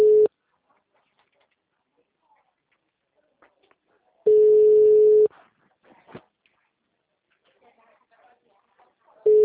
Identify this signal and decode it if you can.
Ringback tone (Europe, 425 Hz).